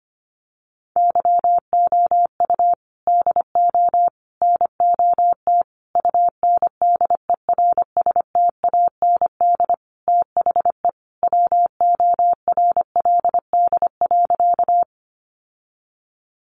Morse code, heard as a keyed tone, about 25 wpm, 700 Hz. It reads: YOU DO NOT UNDERSTAND THE WORLD.
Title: Sentences from Top 600 Words 25wpm